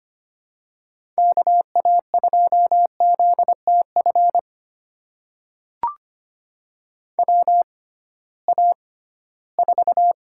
Morse code, heard as a keyed tone, about 25 wpm, spaced as for 16 wpm, 700 Hz. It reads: KA2ZTF WA4
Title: USA Call Signs - 6x Character Spacing - 25wpm